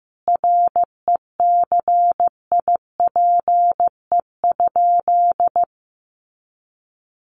Morse code, heard as a keyed tone, about 15 wpm, 700 Hz. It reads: RECIPE?
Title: Sentences from Top 1200 Words 15wpm